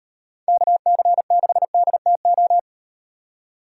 Morse code, heard as a keyed tone, about 38 wpm, 700 Hz. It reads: KC6DTO